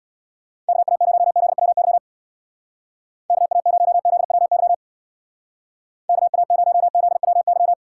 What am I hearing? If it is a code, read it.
Morse code, heard as a keyed tone, about 61 wpm, 700 Hz. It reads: KA0ZWY KA0ZWY KA0ZWY
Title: Speed Racing - USA Call Signs 50wpm